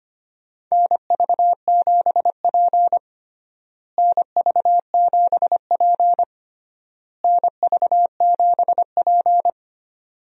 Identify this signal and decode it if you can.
Morse code, heard as a keyed tone, about 25 wpm, 700 Hz. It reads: NV7P NV7P NV7P
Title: USA Call Signs - 4 Characters - Repeated 3x - 25wpm